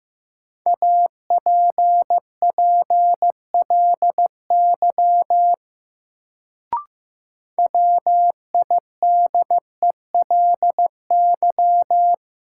Morse code, heard as a keyed tone, about 15 wpm, 700 Hz. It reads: APPLY WIDELY
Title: Sentences from Top 1200 Words - New Word Drill - 15wpm